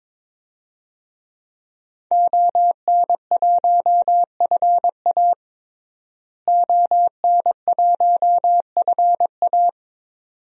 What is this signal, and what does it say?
Morse code, heard as a keyed tone, about 22 wpm, 700 Hz. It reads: ON1FA ON1FA
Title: DXCC-100 Call Signs - 5 Characters - Repeated 2x - 22wpm